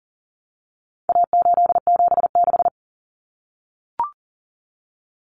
Morse code, heard as a keyed tone, about 40 wpm, 700 Hz. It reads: A876
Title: Common Cut Numbers - 4-Digit 40wpm